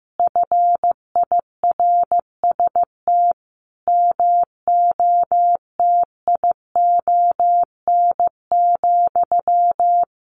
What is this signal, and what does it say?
Morse code, heard as a keyed tone, about 15 wpm, 700 Hz. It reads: FIRST MOTION,
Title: A Sound Of Thunder 15wpm